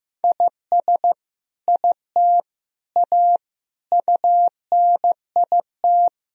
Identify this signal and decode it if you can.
Morse code, heard as a keyed tone, about 16 wpm, 700 Hz. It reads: IS IT A UNIT
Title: Sentences from Top 1000 Words - Review Part 2 - 15wpm